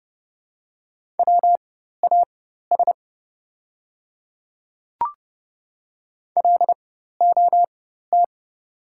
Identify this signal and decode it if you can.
Morse code, heard as a keyed tone, about 30 wpm, spaced as for 14 wpm, 700 Hz. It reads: WAS LOT
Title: 4x Character Spacing - Top 500 Words - 30wpm